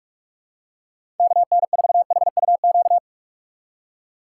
Morse code, heard as a keyed tone, about 47 wpm, 700 Hz. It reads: KN4HUQ
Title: USA Call Signs - 2x Character Spacing - 45wpm